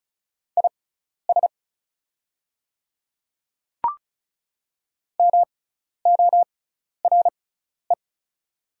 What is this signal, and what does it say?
Morse code, heard as a keyed tone, about 36 wpm, spaced as for 12 wpm, 690 Hz. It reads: IS MORE
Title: 6x Character Spacing - Top 100 Words - 35wpm